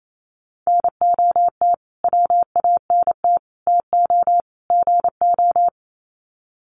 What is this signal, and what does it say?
Morse code, heard as a keyed tone, about 28 wpm, 700 Hz. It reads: NOT WANT TO GO
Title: Sets of 4 Words 28wpm